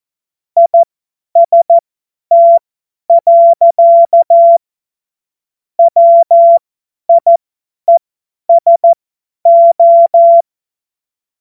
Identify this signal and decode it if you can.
Morse code, heard as a keyed tone, about 14 wpm, spaced as for 10 wpm, 680 Hz. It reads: IST. WIESO
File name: Morsekurs / Witze / Witz 16. November 2021, 50 F70